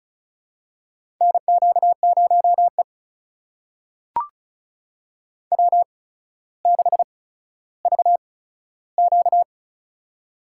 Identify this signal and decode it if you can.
Morse code, heard as a keyed tone, about 36 wpm, spaced as for 15 wpm, 700 Hz. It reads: NQ0E W6VQ
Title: USA Call Signs - 4 Characters - 8x Character Spacing - 35wpm